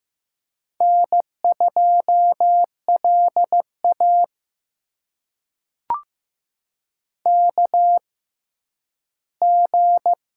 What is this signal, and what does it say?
Morse code, heard as a keyed tone, about 15 wpm, spaced as for 10 wpm, 700 Hz. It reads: N2LA KG
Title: USA Call Signs - 6x Character Spacing - 15wpm